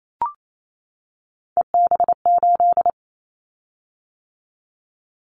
Morse code, heard as a keyed tone, about 28 wpm, 700 Hz. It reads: EB8
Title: Callsign Prefixes for Top 100 DXCC - 28wpm